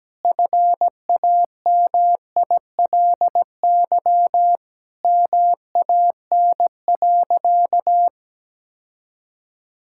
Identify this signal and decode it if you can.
Morse code, heard as a keyed tone, about 17 wpm, 700 Hz. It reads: FAMILY MAN.
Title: Sentences from Top 1000 Words - Review Part 5 - 17wpm